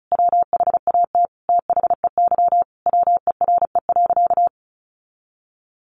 Morse code, heard as a keyed tone, about 34 wpm, 700 Hz. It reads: WHAT THEY WERE.
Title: Sentences from Top 1000 Words 35wpm